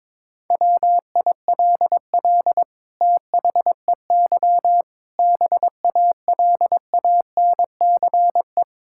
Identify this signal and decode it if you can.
Morse code, heard as a keyed tone, about 22 wpm, 700 Hz. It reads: WILL THEY BALANCE
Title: Sentences from Top 1000 Words - Review Part 6 - 22wpm